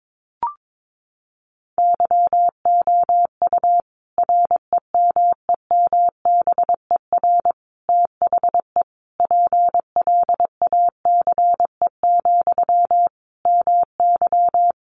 Morse code, heard as a keyed tone, about 22 wpm, 700 Hz. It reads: YOU REMEMBER THE PLACE, MY
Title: Sentences from Top 400 Words 22wpm